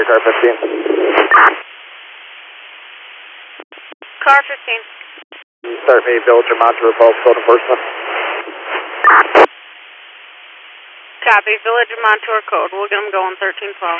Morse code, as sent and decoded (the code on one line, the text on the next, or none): none